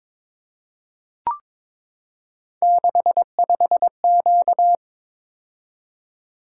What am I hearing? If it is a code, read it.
Morse code, heard as a keyed tone, about 22 wpm, 700 Hz. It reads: 65Q
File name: practice / 22wpm / Three Letter-Number Combinations - 22wpm